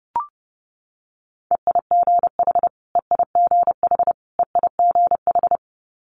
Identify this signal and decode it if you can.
Morse code, heard as a keyed tone, about 31 wpm, 700 Hz. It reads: EIGH EIGH EIGH
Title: Multi-letter Phonemes Repeated 3x 30wpm